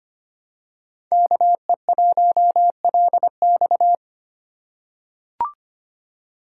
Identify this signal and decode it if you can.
Morse code, heard as a keyed tone, about 25 wpm, 700 Hz. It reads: KE1LX
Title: USA Call Signs - 5 Characters - 2x Character Spacing - 25wpm